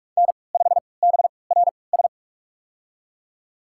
Morse code, heard as a keyed tone, about 46 wpm, spaced as for 26 wpm, 700 Hz. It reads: N5BRS
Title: USA Call Signs - 3x Character Spacing - 45wpm